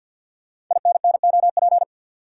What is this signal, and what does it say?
Morse code, heard as a keyed tone, about 50 wpm, 700 Hz. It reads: INNOP
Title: Code Groups - 5 Letters - 50wpm